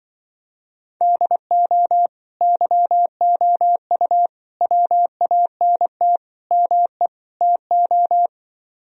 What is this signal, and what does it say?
Morse code, heard as a keyed tone, about 25 wpm, 700 Hz. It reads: DO YOU WANT ME TO